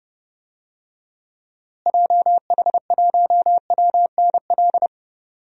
Morse code, heard as a keyed tone, about 30 wpm, 700 Hz. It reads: JH1WNL
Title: DXCC-100 Call Signs - 6 Characters - Repeated 3x - 30wpm